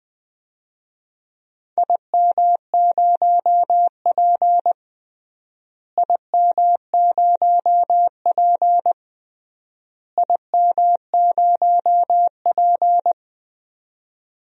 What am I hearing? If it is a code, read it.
Morse code, heard as a keyed tone, about 20 wpm, 700 Hz. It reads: IM0P IM0P IM0P
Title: DXCC-100 Call Signs - 4 Characters - Repeated 3x - 20wpm